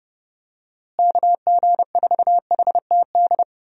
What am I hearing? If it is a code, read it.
Morse code, heard as a keyed tone, about 32 wpm, 700 Hz. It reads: KG4HTD
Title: USA Call Signs - 6 Characters - Repeated 2x - 30wpm